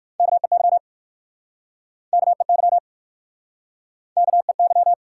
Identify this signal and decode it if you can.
Morse code, heard as a keyed tone, about 56 wpm, 700 Hz. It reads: KEY KEY KEY
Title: Speed-Racing Top 180 QSO Elements 45wpm